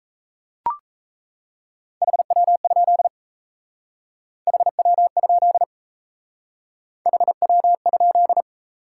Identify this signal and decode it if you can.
Morse code, heard as a keyed tone, about 36 wpm, 700 Hz. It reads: HW? HW? HW?